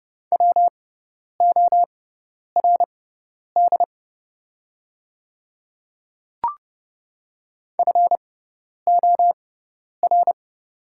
Morse code, heard as a keyed tone, about 31 wpm, spaced as for 10 wpm, 690 Hz. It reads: WORD FOR
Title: 6x Character Spacing - Top 200 Words - 30wpm